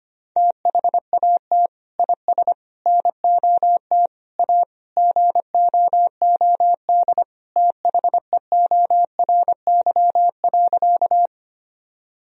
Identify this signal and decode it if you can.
Morse code, heard as a keyed tone, about 25 wpm, 700 Hz. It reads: THAT IS NOT A GOOD THEORY.
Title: Sentences from Top 1000 Words - Review Part 3 - 25wpm